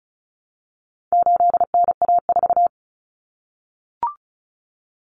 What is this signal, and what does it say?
Morse code, heard as a keyed tone, about 34 wpm, 700 Hz. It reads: 8NA4
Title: Common Cut Numbers - 4-Digit 35wpm